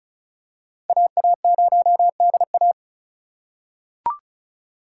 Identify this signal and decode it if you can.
Morse code, heard as a keyed tone, about 34 wpm, 700 Hz. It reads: AA0DA
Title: USA Call Signs - 5 Characters - 2x Character Spacing - 35wpm